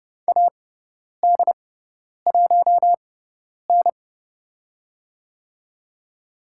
Morse code, heard as a keyed tone, about 30 wpm, spaced as for 10 wpm, 700 Hz. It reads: AD1N
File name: Lesson 18g - Letter m - Call Signs 30wpm